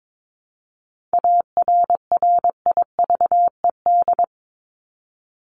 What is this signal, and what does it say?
Morse code, heard as a keyed tone, about 22 wpm, 700 Hz. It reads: ARRIVED
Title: ED Suffix - Top 100 Words with Base Words 22wpm